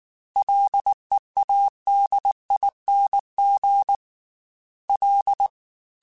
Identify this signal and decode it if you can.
Morse code, heard as a keyed tone, about 19 wpm, 800 Hz. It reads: LEADING L
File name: Ls / Single L m3s / Single L Prefixe 3x2 Twice @18